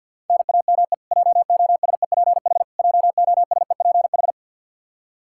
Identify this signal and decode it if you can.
Morse code, heard as a keyed tone, about 53 wpm, 700 Hz. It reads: NAME JOSEPH JOSEPH
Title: Practice QSO - Exchange 1 - Beginning - 50wpm